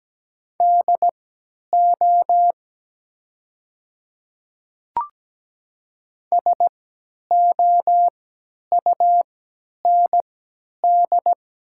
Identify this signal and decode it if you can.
Morse code, heard as a keyed tone, about 17 wpm, spaced as for 10 wpm, 700 Hz. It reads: DO SOUND
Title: 3x Character Spacing - Top 100 Words - 17wpm